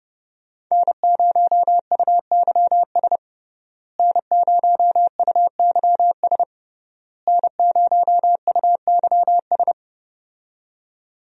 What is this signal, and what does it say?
Morse code, heard as a keyed tone, about 32 wpm, 700 Hz. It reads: N0UYS N0UYS N0UYS
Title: USA Call Signs - 5 Characters - Repeated 3x - 30wpm